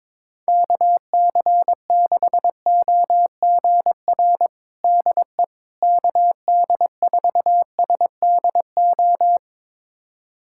Morse code, heard as a keyed tone, about 22 wpm, 700 Hz. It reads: KC6OGR DE KD4SDO
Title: USA Callsign DE Callsign 22wpm